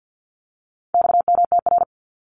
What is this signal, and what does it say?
Morse code, heard as a keyed tone, about 50 wpm, 700 Hz. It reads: XMTR
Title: Abbreviations Repeated 3x - 50wpm